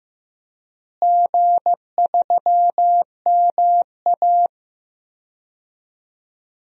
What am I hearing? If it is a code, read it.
Morse code, heard as a keyed tone, about 15 wpm, 700 Hz. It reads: G3MA
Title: DXCC-100 Call Signs - 4 Characters - Repeated 3x - 15wpm